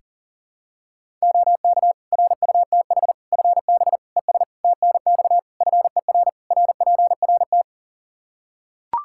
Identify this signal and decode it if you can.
Morse code, heard as a keyed tone, about 40 wpm, 700 Hz. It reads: OK RUTH FB ES TNX FER RPRT
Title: Practice QSO - Exchange 2 - Info - 40wpm